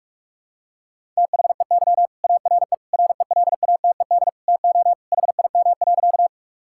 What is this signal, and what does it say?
Morse code, heard as a keyed tone, about 42 wpm, 700 Hz. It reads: THEY ARE RELATED TO HIM.
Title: Sentences from Top 1000 Words - Review Part 5 - 45wpm